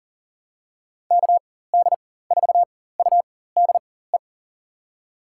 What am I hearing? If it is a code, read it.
Morse code, heard as a keyed tone, about 37 wpm, spaced as for 18 wpm, 700 Hz. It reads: KD4UDE
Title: USA Call Signs - 6 Characters - 4x Character Spacing - 40wpm